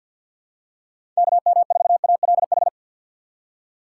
Morse code, heard as a keyed tone, about 51 wpm, 700 Hz. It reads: KM4ALH
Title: USA Call Signs - 6 Characters - 3x Character Spacing - 50wpm